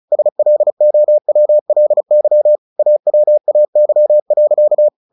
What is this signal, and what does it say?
Morse code, heard as a keyed tone, about 34 wpm, 600 Hz. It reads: SLOWLY AWAY.
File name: the old man and the sea 2 35wpm